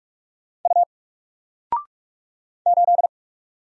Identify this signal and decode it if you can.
Morse code, heard as a keyed tone, about 46 wpm, 700 Hz. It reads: U 8
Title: Single Letter-Number 45wpm